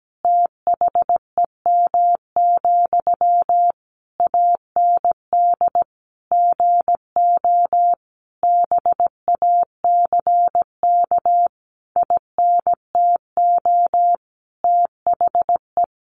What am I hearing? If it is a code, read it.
Morse code, heard as a keyed tone, about 17 wpm, 700 Hz. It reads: THEM, AND GO BACK INTO THE
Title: A Sound Of Thunder 17wpm